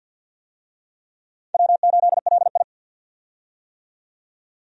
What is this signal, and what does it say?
Morse code, heard as a keyed tone, about 50 wpm, 700 Hz. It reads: W8LI
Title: USA Call Signs - 4 Characters - 50wpm